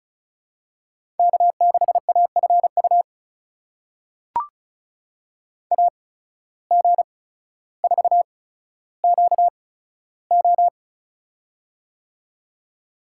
Morse code, heard as a keyed tone, about 34 wpm, spaced as for 15 wpm, 700 Hz. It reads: K6AFU AG4QO
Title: USA Call Signs - 5 Characters - 8x Character Spacing - 35wpm